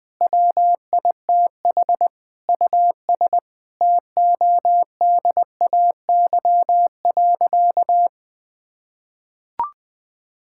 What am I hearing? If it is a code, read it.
Morse code, heard as a keyed tone, about 20 wpm, 700 Hz. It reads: WITH US TODAY.
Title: Sentences with 5-Word Phrases - 20wpm